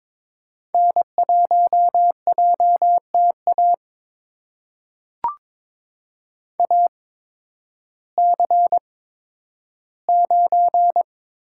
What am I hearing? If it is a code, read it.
Morse code, heard as a keyed tone, about 22 wpm, spaced as for 12 wpm, 700 Hz. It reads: N1JTA AC9
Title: USA Call Signs - 5 Characters - 8x Character Spacing - 22wpm